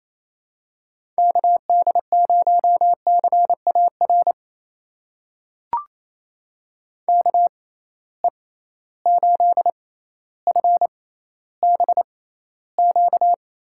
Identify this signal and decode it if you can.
Morse code, heard as a keyed tone, about 28 wpm, spaced as for 14 wpm, 700 Hz. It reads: KD0CAR KE8FBQ
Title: USA Call Signs - 6 Characters - 6x Character Spacing - 28wpm